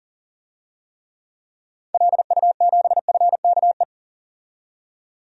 Morse code, heard as a keyed tone, about 40 wpm, 700 Hz. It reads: LU7FKE
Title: DXCC-100 Call Signs - 6 Characters - 40wpm